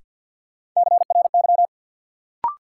Morse code, heard as a keyed tone, about 53 wpm, 700 Hz. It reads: CRY
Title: Sets of 5 Words Encore - New Word Drill - 50wpm